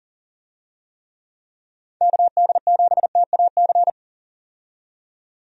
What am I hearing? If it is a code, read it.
Morse code, heard as a keyed tone, about 39 wpm, 700 Hz. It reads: KD7TAC
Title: USA Call Signs - 6 Characters - 40wpm